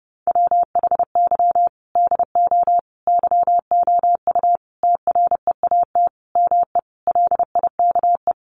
Morse code, heard as a keyed tone, about 30 wpm, 700 Hz. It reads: WHY DO YOU TREAT ME LIKE
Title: Sentences from Top 1300 Words 30wpm